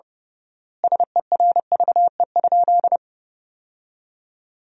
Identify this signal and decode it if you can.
Morse code, heard as a keyed tone, about 30 wpm, 700 Hz. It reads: SERVE?
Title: 3x Word-Spacing - Sentences from Top 400 Words - 30wpm